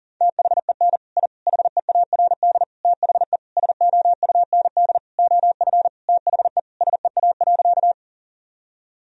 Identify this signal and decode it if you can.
Morse code, heard as a keyed tone, about 37 wpm, 700 Hz. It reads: THEN I HEARD THE SOUND OF THE SEA.
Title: Sentences from Top 1000 Words - Review Part 1 - 40wpm